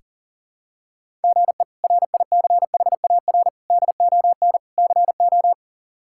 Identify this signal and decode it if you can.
Morse code, heard as a keyed tone, about 37 wpm, 690 Hz. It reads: GE RICHAR DON CO
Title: K1USN SST QSO Practice - 40wpm